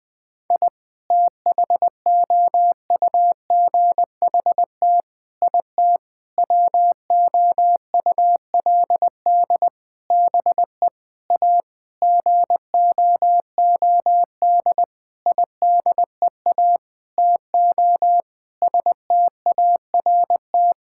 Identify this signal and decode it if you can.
Morse code, heard as a keyed tone, about 20 wpm, 700 Hz. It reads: I THOUGHT IT WOULD BE A GOOD IDEA TO START
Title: Sentences with 9-Word Phrases - 20wpm